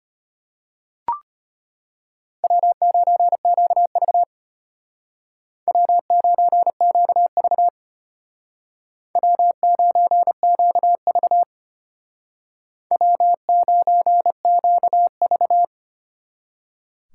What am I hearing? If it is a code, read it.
Morse code, heard as a keyed tone, about 30 wpm, 700 Hz. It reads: W9QV W9QV W9QV W9QV
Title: Speed Racing - USA 1x2 Call Signs 25wpm